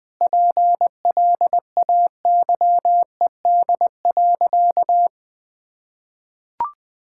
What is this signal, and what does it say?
Morse code, heard as a keyed tone, about 20 wpm, 700 Hz. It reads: PLAYED.
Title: Sentences with 3-Word Phrases - 20wpm